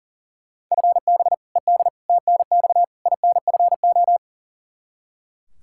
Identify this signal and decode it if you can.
Morse code, heard as a keyed tone, about 40 wpm, 700 Hz. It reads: FB ED TNX INFO
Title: Practice QSO - Exchange 3 - More Info - 40wpm